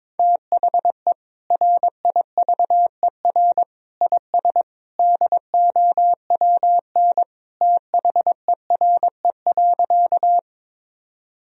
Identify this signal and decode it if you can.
Morse code, heard as a keyed tone, about 22 wpm, 700 Hz. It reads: THE RIVER IS DOWN THERE.
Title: Sentences from Top 1000 Words - Review Part 1 - 22wpm